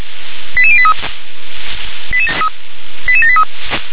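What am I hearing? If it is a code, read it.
Five-tone selective call (ZVEI), heard as quick runs of five tones, tones of about 70 ms. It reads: 88083 88783
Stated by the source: ZVEI five-tone call received by SDR.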